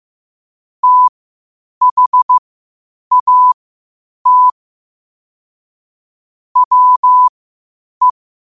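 Morse code, heard as a keyed tone, about 15 wpm, spaced as for 8 wpm, 1000 Hz. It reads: THAT WE
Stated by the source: Morse code quotes at 8 words per minute from QOTD at 8 WPM.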